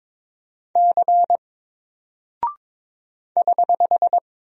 Morse code, heard as a keyed tone, about 21 wpm, 700 Hz. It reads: C <HH>
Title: Single Character (Letter:Number:Punctuation:Prosign) Alt 22wpm